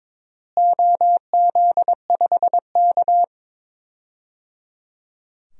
Morse code, heard as a keyed tone, about 22 wpm, 700 Hz. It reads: OZ5K
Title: DXCC-100 Call Signs - 4 Characters - Repeated 3x - 22wpm